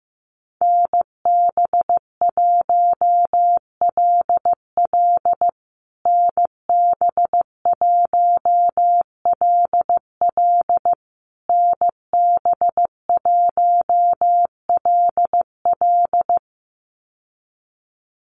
Morse code, heard as a keyed tone, about 15 wpm, 700 Hz. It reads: NB1LL NB1LL NB1LL
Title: USA Call Signs Repeated 3x 15wpm